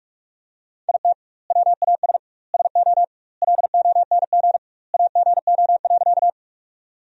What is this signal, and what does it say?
Morse code, heard as a keyed tone, about 42 wpm, 700 Hz. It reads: IT WAS SO LONG AGO.
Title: 2x Word-Spacing - Sentences from Top 500 Words - 45wpm